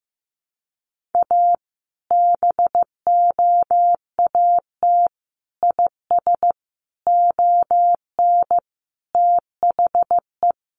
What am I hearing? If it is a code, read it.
Morse code, heard as a keyed tone, about 15 wpm, 700 Hz. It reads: A BOAT IS ON THE
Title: Sentences from Top 1500 Words 15wpm